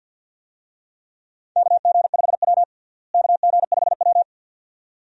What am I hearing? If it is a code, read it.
Morse code, heard as a keyed tone, about 52 wpm, 700 Hz. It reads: KG5W KG5W
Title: USA Call Signs - 4 Characters - Repeated 2x - 50wpm